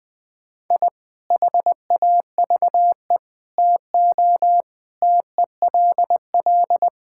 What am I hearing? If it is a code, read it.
Morse code, heard as a keyed tone, about 20 wpm, 690 Hz. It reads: I HAVE TO TELL
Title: Build A Phrase - 5-Word Phrases - 20wpm